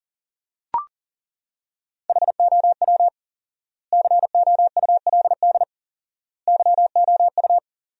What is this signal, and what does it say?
Morse code, heard as a keyed tone, about 37 wpm, 700 Hz. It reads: HOW COULD YOU
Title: 4x Word-Spacing - Sentences from Top 300 Words - 40wpm